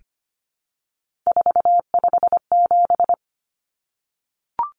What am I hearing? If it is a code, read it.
Morse code, heard as a keyed tone, about 25 wpm, 700 Hz. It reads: 457
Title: Numbers - 3-Digit 25wpm